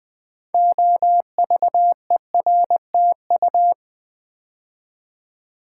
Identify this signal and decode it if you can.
Morse code, heard as a keyed tone, about 20 wpm, 700 Hz. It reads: OVERTU
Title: Build A Word - 2 Letters At A Time - 8-Letter Words - 20wpm